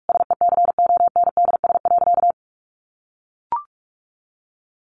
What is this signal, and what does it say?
Morse code, heard as a keyed tone, about 44 wpm, 700 Hz. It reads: SECONDS.
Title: A Sound Of Thunder 45wpm